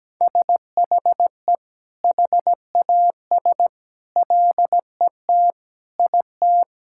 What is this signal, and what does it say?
Morse code, heard as a keyed tone, about 17 wpm, 700 Hz. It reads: SHE HAS LET IT
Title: Sentences from Top 1000 Words - Review Part 6 - 17wpm